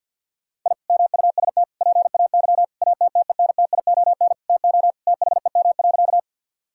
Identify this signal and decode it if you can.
Morse code, heard as a keyed tone, about 50 wpm, 700 Hz. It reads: I MUST PAY ATTENTION TO THEM.
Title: Sentences from Top 1000 Words - Review Part 6 - 50wpm